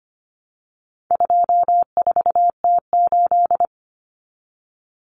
Morse code, heard as a keyed tone, about 25 wpm, 700 Hz. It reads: 24T8
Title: Common Cut Numbers - 4-Digit 25wpm